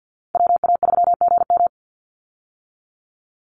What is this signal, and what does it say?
Morse code, heard as a keyed tone, about 50 wpm, 700 Hz. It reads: WA3GM
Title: Call Signs - 50wpm